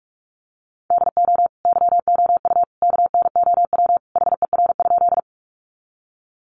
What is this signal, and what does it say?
Morse code, heard as a keyed tone, about 44 wpm, 700 Hz. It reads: DO YOU KNOW HER?
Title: Sentences from Top 100 Words 45wpm